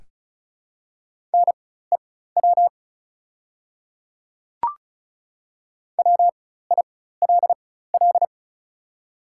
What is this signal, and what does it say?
Morse code, heard as a keyed tone, about 36 wpm, spaced as for 16 wpm, 700 Hz. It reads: NEW WILL
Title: 4x Character Spacing - Top 200 Words - 35wpm